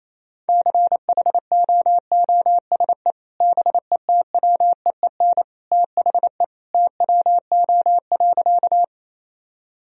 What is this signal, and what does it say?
Morse code, heard as a keyed tone, about 28 wpm, 700 Hz. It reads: CHOOSE BETWEEN THE TWO.
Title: Sentences from Top 1000 Words - Review Part 4 - 28wpm